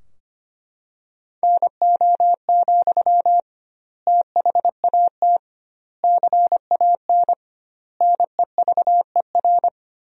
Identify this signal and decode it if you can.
Morse code, heard as a keyed tone, about 25 wpm, 700 Hz. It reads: NO, THAT CAN NEVER
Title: 2x Word-Spacing - Sentences from Top 300 Words - 25wpm